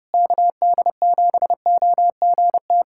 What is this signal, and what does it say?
Morse code, heard as a keyed tone, about 30 wpm, 700 Hz. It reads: KD7OGT
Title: USA Callsign DE Callsign - 6 Characters - 30wpm